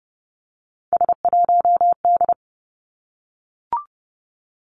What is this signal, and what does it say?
Morse code, heard as a keyed tone, about 30 wpm, 700 Hz. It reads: S1D
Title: Three Challenging Letter-Number Combinations 30wpm